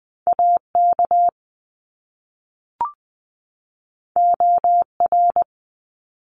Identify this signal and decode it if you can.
Morse code, heard as a keyed tone, about 20 wpm, 700 Hz. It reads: AK OR
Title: ARRL Sections 20wpm